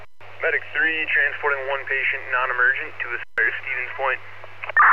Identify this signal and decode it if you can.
Emergency Alert System (SAME) tones, heard as harsh buzzing bursts, with speech. It is heard from 0.3 to 3.9 s.